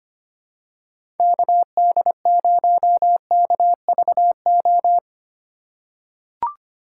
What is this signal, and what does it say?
Morse code, heard as a keyed tone, about 25 wpm, 700 Hz. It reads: KD0KVO
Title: USA Call Signs - 3x Character Spacing - 25wpm